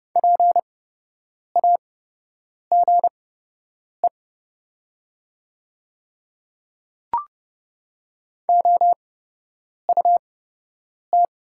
Morse code, heard as a keyed tone, about 30 wpm, spaced as for 8 wpm, 700 Hz. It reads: PAGE OUT
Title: 8x Character Spacing - Top 300 Words - 30wpm